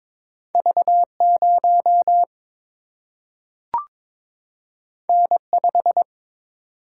Morse code, heard as a keyed tone, about 23 wpm, 700 Hz. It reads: V 0 N 5
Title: Single-Letter Single-Number 22wpm